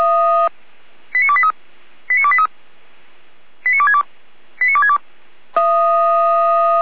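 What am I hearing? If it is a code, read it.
Five-tone selective call (ZVEI), heard as quick runs of five tones, tones of about 70 ms. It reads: 89283 89283 89262 89262
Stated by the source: Received by SDR, ZVEI five-tone selective call.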